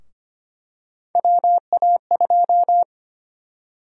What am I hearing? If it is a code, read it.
Morse code, heard as a keyed tone, about 25 wpm, 700 Hz. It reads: WA2
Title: USA Call Sign Prefixes - 25wpm